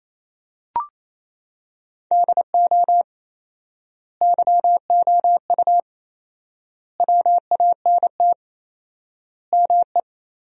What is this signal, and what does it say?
Morse code, heard as a keyed tone, about 28 wpm, 700 Hz. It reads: DO YOU WANT ME
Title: 4x Word-Spacing - Sentences from Top 200 Words - 28wpm